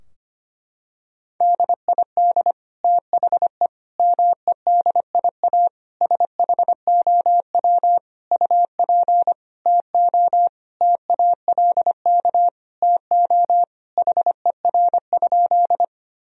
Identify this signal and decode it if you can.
Morse code, heard as a keyed tone, about 25 wpm, 700 Hz. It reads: DID THE MEDIA SHOW UP TO TALK TO HER?
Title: Sentences from Top 1000 Words - Review Part 4 - 25wpm